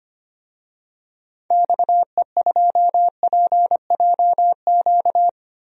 Morse code, heard as a keyed tone, about 25 wpm, 700 Hz. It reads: XE2PJQ
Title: DXCC-100 Call Signs - 6 Characters - 25wpm